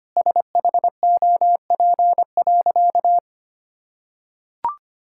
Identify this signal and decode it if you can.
Morse code, heard as a keyed tone, about 25 wpm, 700 Hz. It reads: SHOP.
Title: Sentences from Top 1600 Words 25wpm